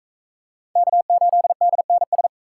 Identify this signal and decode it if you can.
Morse code, heard as a keyed tone, about 39 wpm, 700 Hz. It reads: K8DNS